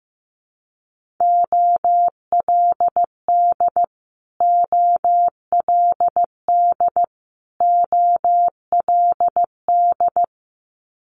Morse code, heard as a keyed tone, about 15 wpm, 700 Hz. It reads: OLD OLD OLD
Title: Multi-letter Phonemes Repeated 3x 15wpm